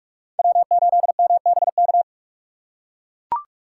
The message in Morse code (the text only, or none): W8MBK